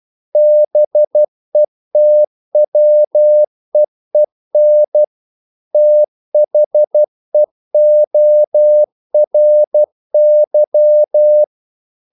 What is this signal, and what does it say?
Morse code, heard as a keyed tone, about 12 wpm, 600 Hz. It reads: BETWEEN THEORY
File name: CW / Practice / EnglishQuotations0112